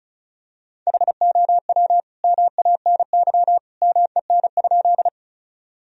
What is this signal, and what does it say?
Morse code, heard as a keyed tone, about 36 wpm, 700 Hz. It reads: HOW MANY MEN?
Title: Sentences from Top 1000 Words - Review Part 1 - 35wpm